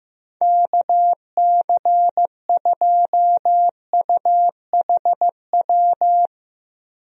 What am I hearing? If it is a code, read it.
Morse code, heard as a keyed tone, about 15 wpm, 700 Hz. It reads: KC2UHW
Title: USA Call Signs - 6 Characters - 15wpm